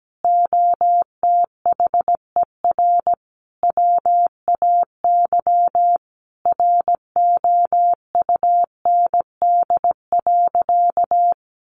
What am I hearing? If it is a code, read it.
Morse code, heard as a keyed tone, about 17 wpm, 700 Hz. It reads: OTHER WAY ROUND.
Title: Sentences from Top 200 Words 17wpm